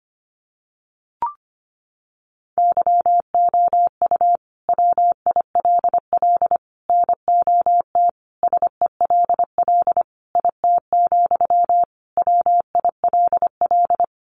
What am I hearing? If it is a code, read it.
Morse code, heard as a keyed tone, about 25 wpm, 700 Hz. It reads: YOU WILL NOT SELL IT, WILL
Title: Sentences from Top 1100 Words 25wpm